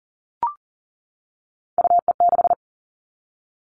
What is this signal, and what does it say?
Morse code, heard as a keyed tone, about 40 wpm, 700 Hz. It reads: UE6
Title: All Cut Numbers - 3-Digit 40wpm